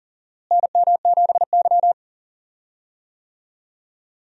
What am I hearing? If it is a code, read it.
Morse code, heard as a keyed tone, about 40 wpm, 700 Hz. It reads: NM7Y
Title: USA Call Signs - 4 Characters - Repeated 2x - 40wpm